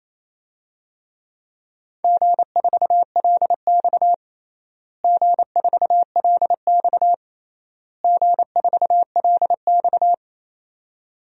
Morse code, heard as a keyed tone, about 28 wpm, 700 Hz. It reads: G4LX G4LX G4LX
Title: DXCC-100 Call Signs - 4 Characters - Repeated 3x - 28wpm